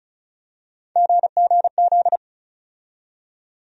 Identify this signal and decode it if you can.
Morse code, heard as a keyed tone, about 36 wpm, 700 Hz. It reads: GGZ
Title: USA Call Sign Postfixes - 3 Letters 35wpm